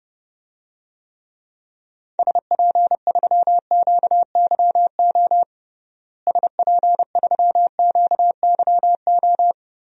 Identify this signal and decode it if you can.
Morse code, heard as a keyed tone, about 29 wpm, 700 Hz. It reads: SP3QYO SP3QYO
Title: DXCC-100 Call Signs - 6 Characters - Repeated 3x - 30wpm